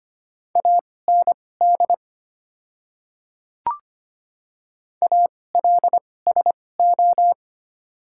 Morse code, heard as a keyed tone, about 25 wpm, spaced as for 18 wpm, 700 Hz. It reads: AND ALSO
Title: 2x Character Spacing - Top 200 Words - 25wpm